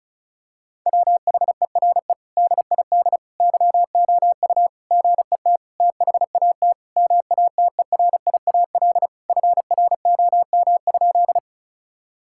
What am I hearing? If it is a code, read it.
Morse code, heard as a keyed tone, about 36 wpm, 700 Hz. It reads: WHERE DID YOU GET THAT MATERIAL FROM?
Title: Sentences from Top 1000 Words - Review Part 5 - 35wpm